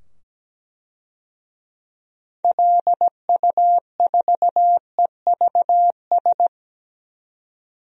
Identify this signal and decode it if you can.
Morse code, heard as a keyed tone, about 17 wpm, 700 Hz. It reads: LU4EVS
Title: DXCC-100 Call Signs - 6 Characters - 17wpm